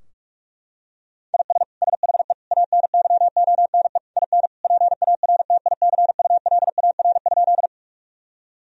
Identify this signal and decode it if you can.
Morse code, heard as a keyed tone, about 46 wpm, 700 Hz. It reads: IS SHE ANYONE IN PARTICULAR?
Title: Sentences from Top 1000 Words - Review Part 4 - 45wpm